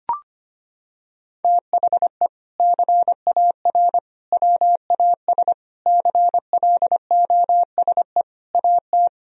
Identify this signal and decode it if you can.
Morse code, heard as a keyed tone, about 25 wpm, 700 Hz. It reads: THE CAR WAS CLOSE AT
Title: Sentences from Top 1000 Words - Review Part 1 - 25wpm